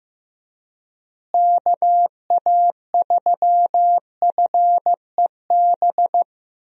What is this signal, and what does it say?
Morse code, heard as a keyed tone, about 15 wpm, 700 Hz. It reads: KA3FEB
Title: USA Call Signs - 6 Characters - Repeated 3x - 15wpm